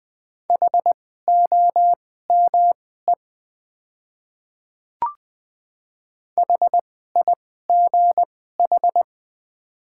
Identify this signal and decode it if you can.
Morse code, heard as a keyed tone, about 20 wpm, spaced as for 15 wpm, 690 Hz. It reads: HOME HIGH
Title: 2x Character Spacing - Top 200 Words - 20wpm